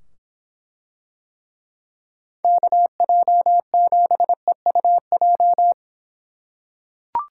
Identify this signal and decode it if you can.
Morse code, heard as a keyed tone, about 24 wpm, 700 Hz. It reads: KJ7EUJ